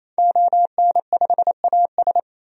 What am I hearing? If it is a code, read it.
Morse code, heard as a keyed tone, about 28 wpm, 700 Hz. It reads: ON5AS
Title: DXCC-100 Call Signs - 5 Characters - 28wpm